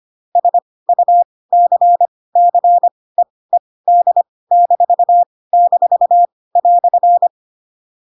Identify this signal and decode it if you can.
Morse code, heard as a keyed tone, about 25 wpm, spaced as for 18 wpm, 700 Hz. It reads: SUCCEED--"